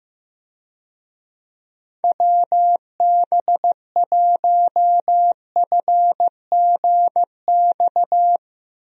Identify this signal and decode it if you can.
Morse code, heard as a keyed tone, about 15 wpm, 700 Hz. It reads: WB1FGX
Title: USA Call Signs - 6 Characters - Repeated 3x - 15wpm